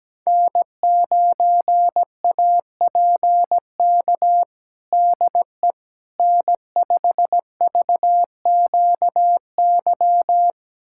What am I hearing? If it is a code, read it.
Morse code, heard as a keyed tone, about 17 wpm, 700 Hz. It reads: N9APK DE N5VQY
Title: USA Callsign DE Callsign - 5 Characters - 17wpm